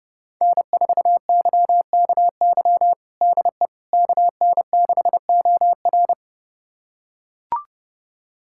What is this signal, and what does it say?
Morse code, heard as a keyed tone, about 30 wpm, 700 Hz. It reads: N4YKY DE KN6OR
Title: USA Callsign DE Callsign 30wpm